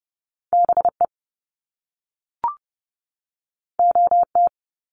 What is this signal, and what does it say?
Morse code, heard as a keyed tone, about 30 wpm, 700 Hz. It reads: BE OT
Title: Top Two Letter Combinations in Words 30wpm